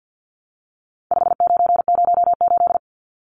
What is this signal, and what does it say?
Morse code, heard as a keyed tone, about 50 wpm, 700 Hz. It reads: 5908
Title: Numbers - 4-Digit 50wpm